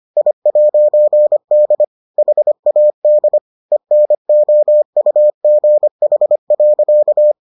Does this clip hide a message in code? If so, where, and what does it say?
Yes: Morse code from 0.2 to 7.3 s, heard as a keyed tone, about 25 wpm, 600 Hz. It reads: I'D HAD ENOUGH.